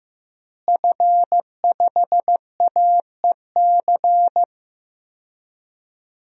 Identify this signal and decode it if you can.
Morse code, heard as a keyed tone, about 16 wpm, 700 Hz. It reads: F5AEC
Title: DXCC-100 Call Signs - 5 Characters - Repeated 3x - 15wpm